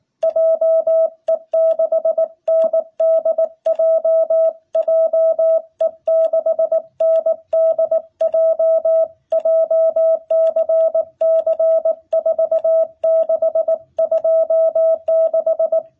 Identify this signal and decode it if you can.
Morse code, heard as a keyed tone, about 19 wpm, 660 Hz. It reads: JE6NDJ JE6NDJ JCC4626
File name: jh8cbh / 2021cw